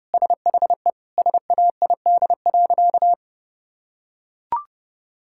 Morse code, heard as a keyed tone, about 32 wpm, 700 Hz. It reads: SHE SAID.
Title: Sentences from Top 1000 Words - Review Part 6 - 30wpm